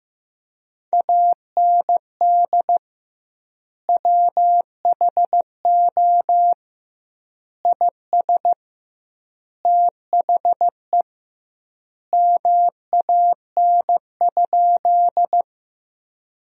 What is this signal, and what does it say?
Morse code, heard as a keyed tone, about 15 wpm, 700 Hz. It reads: AND WHO IS THE MAN?
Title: 2x Word-Spacing - Sentences from Top 200 Words - 15wpm